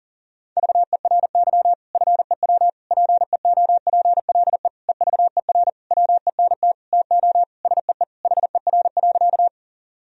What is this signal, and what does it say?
Morse code, heard as a keyed tone, about 40 wpm, 700 Hz. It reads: VERY FEW PEOPLE EVER WENT TO SEE HER.
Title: Sentences from Top 1000 Words - Review Part 1 - 40wpm